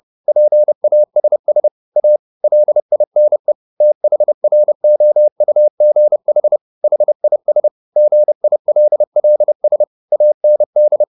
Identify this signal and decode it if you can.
Morse code, heard as a keyed tone, about 30 wpm, 600 Hz. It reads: PASS A LINE THROUGH HIS GILLS AND